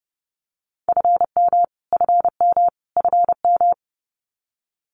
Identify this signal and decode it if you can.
Morse code, heard as a keyed tone, about 30 wpm, 700 Hz. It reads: FM FM FM
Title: Abbreviations Repeated 3x - 30wpm